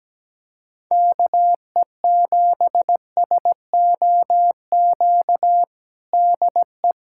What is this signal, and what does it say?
Morse code, heard as a keyed tone, about 17 wpm, 700 Hz. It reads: KE7SOQ DE
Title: USA Callsign DE Callsign 17wpm